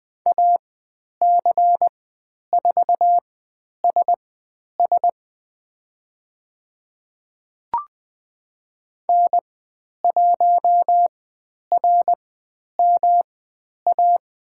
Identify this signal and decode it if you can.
Morse code, heard as a keyed tone, about 20 wpm, spaced as for 10 wpm, 700 Hz. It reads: AC4SS N1RMA
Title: 109 Lesson 18g - Letter m - Call Signs 20wpm